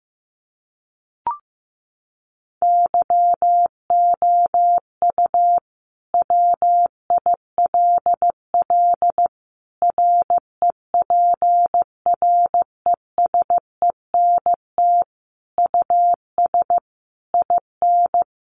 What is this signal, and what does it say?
Morse code, heard as a keyed tone, about 15 wpm, 700 Hz. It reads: YOU WILL REPRESENT US IN
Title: Sentences from Top 1200 Words 15wpm